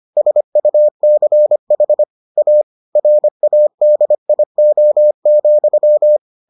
Morse code, heard as a keyed tone, about 25 wpm, 600 Hz. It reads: SUCH A RADIO,